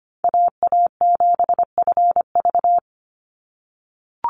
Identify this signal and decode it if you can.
Morse code, heard as a keyed tone, about 25 wpm, 700 Hz. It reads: AA7FV
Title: Call Signs - 25wpm